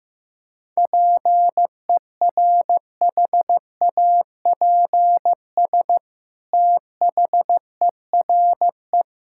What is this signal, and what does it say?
Morse code, heard as a keyed tone, about 15 wpm, 700 Hz. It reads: PERHAPS THERE
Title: Sentences from Top 1000 Words - Review Part 5 - 15wpm